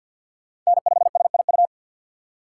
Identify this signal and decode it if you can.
Morse code, heard as a keyed tone, about 53 wpm, 700 Hz. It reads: N5SIU
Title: USA Call Signs - 2x Character Spacing - 50wpm